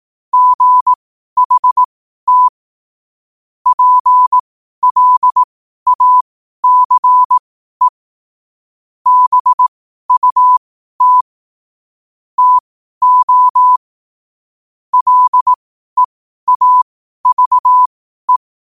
Morse code, heard as a keyed tone, about 18 wpm, spaced as for 12 wpm, 1000 Hz. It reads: GHT PLACE BUT TO LEAVE
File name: qotdfiles / 12wpm